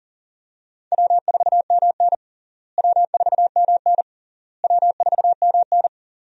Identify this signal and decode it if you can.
Morse code, heard as a keyed tone, about 40 wpm, 700 Hz. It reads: W4MN W4MN W4MN
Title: USA Call Signs - 4 Characters - Repeated 3x - 40wpm